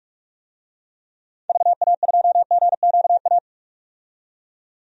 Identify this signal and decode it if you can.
Morse code, heard as a keyed tone, about 44 wpm, 700 Hz. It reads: VA2GQA
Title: DXCC-100 Call Signs - 6 Characters - 45wpm